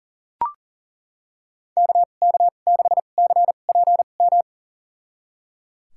Morse code, heard as a keyed tone, about 39 wpm, spaced as for 29 wpm, 700 Hz. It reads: KK6CPM
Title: USA Call Signs - 6 Characters - 2x Character Spacing - 40wpm